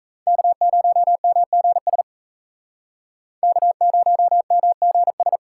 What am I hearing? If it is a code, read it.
Morse code, heard as a keyed tone, about 35 wpm, 700 Hz. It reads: K0MGS K0MGS
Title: Speed Racing - USA 1x3 Call Signs 28wpm